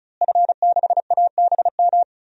Morse code, heard as a keyed tone, about 36 wpm, 700 Hz. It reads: F6ABM
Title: DXCC-100 Call Signs - 5 Characters - Repeated 3x - 35wpm